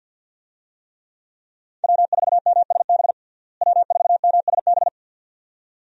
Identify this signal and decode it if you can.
Morse code, heard as a keyed tone, about 50 wpm, 700 Hz. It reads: W4MSB W4MSB
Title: USA Call Signs - 5 Characters - Repeated 2x - 50wpm